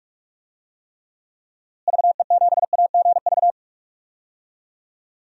Morse code, heard as a keyed tone, about 46 wpm, 700 Hz. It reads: VE7AGV
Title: DXCC-100 Call Signs - 6 Characters - Repeated 3x - 45wpm